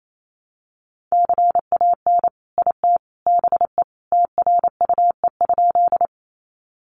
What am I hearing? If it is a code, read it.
Morse code, heard as a keyed tone, about 28 wpm, 700 Hz. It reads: CAN IT BE TRUE?
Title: Sentences from Top 400 Words 28wpm